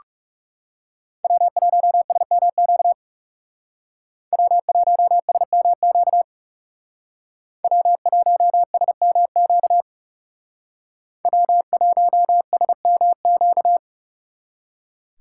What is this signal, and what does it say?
Morse code, heard as a keyed tone, about 36 wpm, 700 Hz. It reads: W1SMQ W1SMQ W1SMQ W1SMQ
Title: Speed Racing - USA 1x3 Call Signs 30wpm